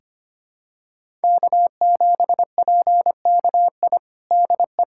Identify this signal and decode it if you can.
Morse code, heard as a keyed tone, about 25 wpm, 700 Hz. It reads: K7PKI DE
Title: USA Callsign DE Callsign - 5 Characters - 25wpm